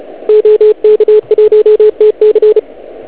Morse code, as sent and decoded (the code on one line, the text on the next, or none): --- -.- .---- - -.-.
OK1TC